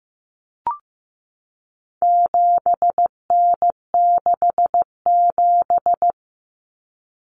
Morse code, heard as a keyed tone, about 15 wpm, 700 Hz. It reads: 7N67
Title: Common Cut Numbers - 4-Digit 15wpm